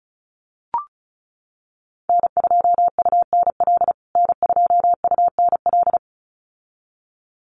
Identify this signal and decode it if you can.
Morse code, heard as a keyed tone, about 34 wpm, 700 Hz. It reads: N2UNL N2UNL
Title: USA Call Signs Repeated 2x 35wpm